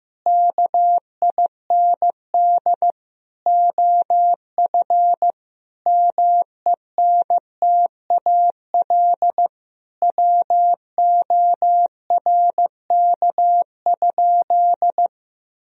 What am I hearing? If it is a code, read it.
Morse code, heard as a keyed tone, about 15 wpm, 700 Hz. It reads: KIND OF MENTAL WORK?
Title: Sentences from Top 1000 Words - Review Part 4 - 15wpm